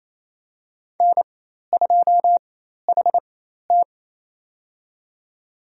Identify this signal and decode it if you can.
Morse code, heard as a keyed tone, about 27 wpm, spaced as for 13 wpm, 700 Hz. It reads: N2HT
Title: USA Call Signs - 4 Characters - 4x Character Spacing - 28wpm